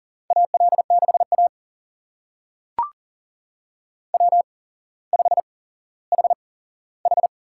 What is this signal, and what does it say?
Morse code, heard as a keyed tone, about 40 wpm, spaced as for 17 wpm, 700 Hz. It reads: AL6A W5HH
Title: USA Call Signs - 4 Characters - 8x Character Spacing - 40wpm